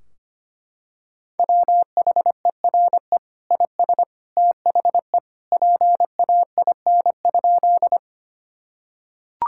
Morse code, heard as a keyed tone, about 25 wpm, 700 Hz. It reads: WHERE IS THE PAIN?
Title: Sentences from Top 1600 Words 25wpm